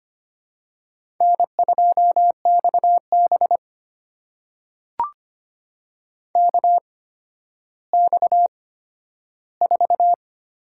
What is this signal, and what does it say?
Morse code, heard as a keyed tone, about 25 wpm, spaced as for 12 wpm, 700 Hz. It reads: N2XB KX4
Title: USA Call Signs - 4 Characters - 8x Character Spacing - 25wpm